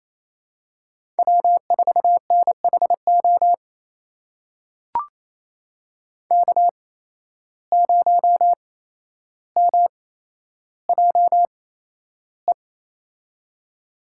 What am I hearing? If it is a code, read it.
Morse code, heard as a keyed tone, about 28 wpm, spaced as for 12 wpm, 700 Hz. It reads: W4NHO K0MJE
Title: USA Call Signs - 5 Characters - 8x Character Spacing - 28wpm